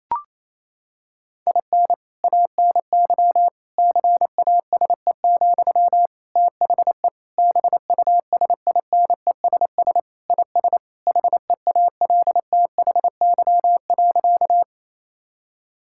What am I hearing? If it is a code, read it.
Morse code, heard as a keyed tone, about 28 wpm, 700 Hz. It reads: IN ANY CASE, THE BUSINESS IS HEALTHY.
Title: Sentences from Top 1000 Words - Review Part 5 - 28wpm